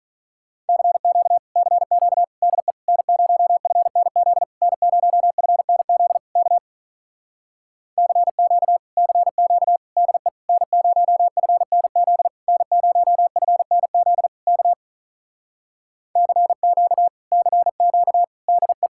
Morse code, heard as a keyed tone, about 45 wpm, 700 Hz. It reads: CQ CQ DE N0FNZ N0FNZ K CQ CQ DE N0FNZ N0FNZ K CQ CQ DE